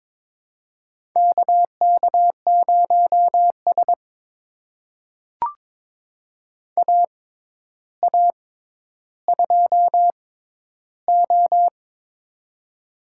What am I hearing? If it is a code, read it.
Morse code, heard as a keyed tone, about 22 wpm, spaced as for 11 wpm, 700 Hz. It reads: KK0S AA2O
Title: USA Call Signs - 4 Characters - 6x Character Spacing - 22wpm